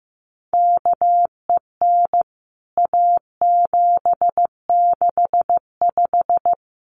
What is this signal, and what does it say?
Morse code, heard as a keyed tone, about 15 wpm, 700 Hz. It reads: KEN A765
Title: CWT Exchange 15wpm